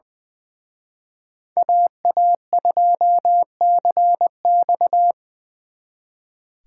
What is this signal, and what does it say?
Morse code, heard as a keyed tone, about 24 wpm, 700 Hz. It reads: AA2CX
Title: Speed Racing - USA 2x2 Call Signs 20wpm